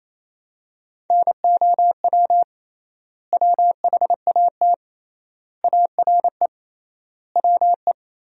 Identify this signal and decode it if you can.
Morse code, heard as a keyed tone, about 27 wpm, 700 Hz. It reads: NOW WHAT ARE WE
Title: 3x Word-Spacing - Sentences from Top 100 Words - 28wpm